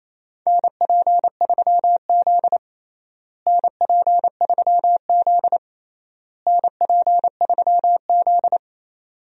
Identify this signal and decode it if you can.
Morse code, heard as a keyed tone, about 28 wpm, 700 Hz. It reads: NP3Z NP3Z NP3Z
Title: USA Call Signs - 4 Characters - Repeated 3x - 28wpm